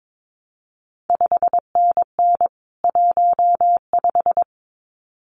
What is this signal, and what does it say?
Morse code, heard as a keyed tone, about 22 wpm, 700 Hz. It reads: 5NN 15
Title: CQ WW DX Contest Exchange - 22wpm